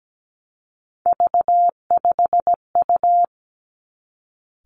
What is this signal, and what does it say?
Morse code, heard as a keyed tone, about 17 wpm, 700 Hz. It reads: V5U
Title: Three Challenging Letter-Number Combinations 17wpm